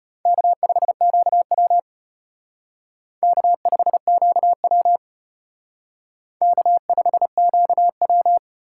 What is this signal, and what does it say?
Morse code, heard as a keyed tone, about 32 wpm, 700 Hz. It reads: K5QW K5QW K5QW